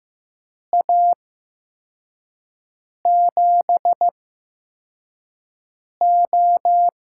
Morse code, heard as a keyed tone, about 15 wpm, spaced as for 4 wpm, 700 Hz. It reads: A7O
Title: USA Call Signs - 8x Character Spacing - 15wpm